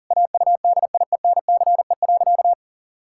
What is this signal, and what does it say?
Morse code, heard as a keyed tone, about 40 wpm, 700 Hz. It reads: AUDIENCE.
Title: Pangram Challenge v2 - 40wpm